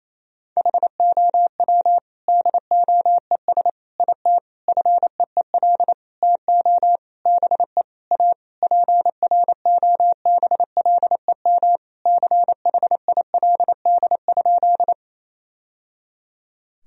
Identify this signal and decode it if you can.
Morse code, heard as a keyed tone, about 28 wpm, 700 Hz. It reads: HOW DOES IT FEEL TO BE A PROBLEM CHILD?
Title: Sentences from Top 1000 Words - Review Part 2 - 28wpm